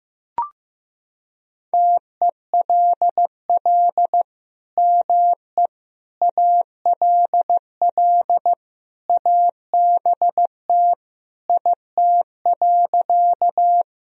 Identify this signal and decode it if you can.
Morse code, heard as a keyed tone, about 16 wpm, 700 Hz. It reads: TELL ME ALL ABT IT.
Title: Sentences from Top 200 Words and Abbreviations 15wpm